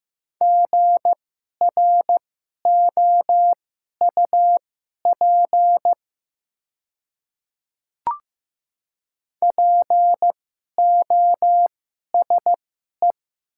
Morse code, heard as a keyed tone, about 15 wpm, spaced as for 11 wpm, 700 Hz. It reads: GROUP POSE
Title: 2x Character Spacing - Top 500 Words - 15wpm